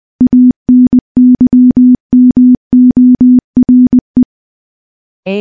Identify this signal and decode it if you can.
Morse code, heard as a keyed tone, about 21 wpm, 250 Hz. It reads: ANYMORE
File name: morse-training-20WPM